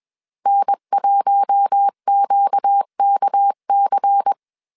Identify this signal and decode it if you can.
Morse code, heard as a keyed tone, about 21 wpm, 800 Hz. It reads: N1QKC
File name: CW / DET / 1 6 1500C S C 20wpm 2116